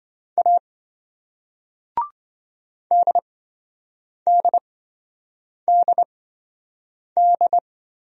Morse code, heard as a keyed tone, about 28 wpm, spaced as for 7 wpm, 700 Hz. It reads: A DDDD